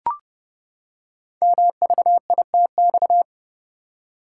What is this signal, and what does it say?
Morse code, heard as a keyed tone, about 30 wpm, 700 Hz. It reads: MVITX
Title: Code Groups - 5 Letters - 30wpm